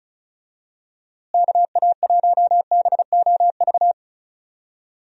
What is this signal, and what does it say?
Morse code, heard as a keyed tone, about 34 wpm, 700 Hz. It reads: KA1BOV
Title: USA Call Signs - 6 Characters - Repeated 3x - 35wpm